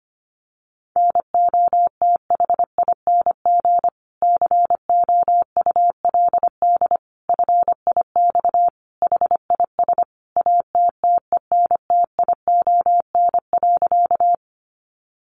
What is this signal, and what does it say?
Morse code, heard as a keyed tone, about 25 wpm, 700 Hz. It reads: NOTHING COULD FIX HIS ATTENTION.
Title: Sentences from Top 1400 Words 25wpm